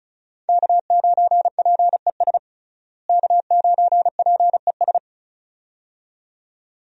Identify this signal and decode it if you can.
Morse code, heard as a keyed tone, about 36 wpm, 700 Hz. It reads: K9PES K9PES
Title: USA Call Signs - 5 Characters - Repeated 2x - 35wpm